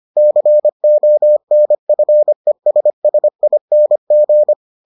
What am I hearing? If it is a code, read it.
Morse code, heard as a keyed tone, about 25 wpm, 600 Hz. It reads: CONFESSING